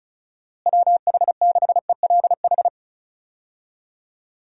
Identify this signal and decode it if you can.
Morse code, heard as a keyed tone, about 36 wpm, 700 Hz. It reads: WH6ELH
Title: USA Call Signs - 6 Characters - Repeated 2x - 35wpm